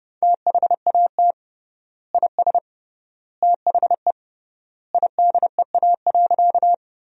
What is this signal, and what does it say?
Morse code, heard as a keyed tone, about 31 wpm, 700 Hz. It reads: THAT IS THE IDEA.
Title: 3x Word-Spacing - Sentences from Top 300 Words - 30wpm